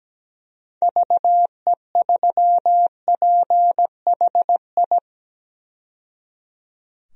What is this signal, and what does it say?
Morse code, heard as a keyed tone, about 17 wpm, 700 Hz. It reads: VE3PHI
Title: DXCC-100 Call Signs - 6 Characters - 17wpm